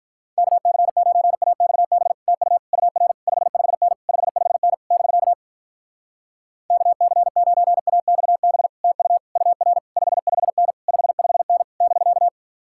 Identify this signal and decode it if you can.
Morse code, heard as a keyed tone, about 48 wpm, 700 Hz. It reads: KC9AXB TU UR 55N 55N <BK> KC9AXB TU UR 55N 55N <BK>